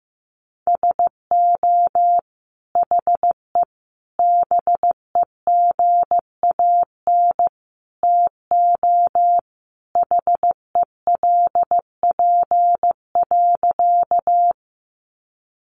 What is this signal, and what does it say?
Morse code, heard as a keyed tone, about 14 wpm, 700 Hz. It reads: SO HE BEGAN TO HELP.
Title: Sentences from Top 300 Words 15wpm